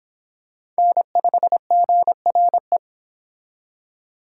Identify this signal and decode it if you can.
Morse code, heard as a keyed tone, about 26 wpm, 700 Hz. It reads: N5GRE